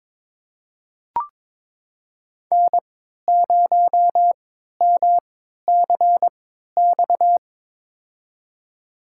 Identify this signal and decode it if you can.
Morse code, heard as a keyed tone, about 22 wpm, spaced as for 12 wpm, 700 Hz. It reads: N0MCX
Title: USA Call Signs - 5 Characters - 3x Character Spacing - 22wpm